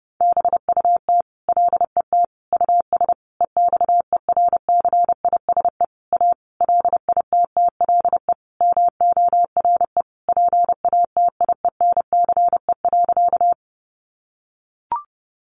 Morse code, heard as a keyed tone, about 30 wpm, 700 Hz. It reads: BUT LET US EXERCISE A LITTLE MORE PATIENCE.
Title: Sentences from Top 1300 Words 30wpm